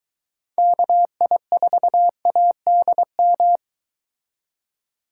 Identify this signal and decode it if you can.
Morse code, heard as a keyed tone, about 22 wpm, 700 Hz. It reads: KI4ADM